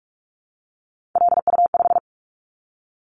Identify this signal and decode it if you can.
Morse code, heard as a keyed tone, about 46 wpm, 700 Hz. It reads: LU5
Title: Callsign Prefixes for Top 50 DXCC - 45wpm